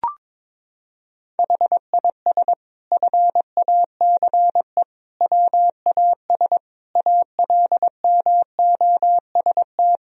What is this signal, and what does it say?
Morse code, heard as a keyed tone, about 22 wpm, 700 Hz. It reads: HIS FACE WAS ALMOST
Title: Sentences from Top 1000 Words - Review Part 4 - 22wpm